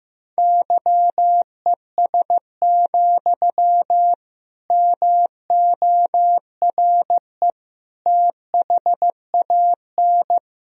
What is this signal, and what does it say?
Morse code, heard as a keyed tone, about 15 wpm, 700 Hz. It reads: YES, MORE THAN
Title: Sentences from Top 1000 Words - Review Part 1 - 15wpm